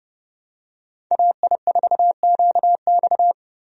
Morse code, heard as a keyed tone, about 30 wpm, 700 Hz. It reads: AI4QX
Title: USA Call Signs - 5 Characters - Repeated 2x - 30wpm